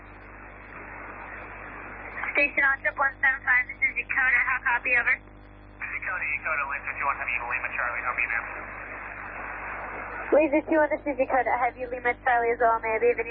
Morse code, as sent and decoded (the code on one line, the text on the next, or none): none